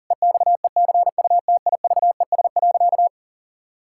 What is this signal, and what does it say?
Morse code, heard as a keyed tone, about 40 wpm, 700 Hz. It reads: EXECUTIVES.
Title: Pangram Challenge v2 - 40wpm